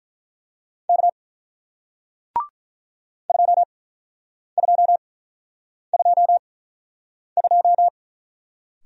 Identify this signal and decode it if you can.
Morse code, heard as a keyed tone, about 47 wpm, spaced as for 9 wpm, 700 Hz. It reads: K 2222